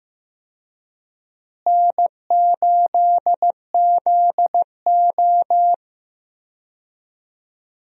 Morse code, heard as a keyed tone, about 15 wpm, 700 Hz. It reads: N8ZO
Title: USA Call Signs - 4 Characters - Repeated 2x - 15wpm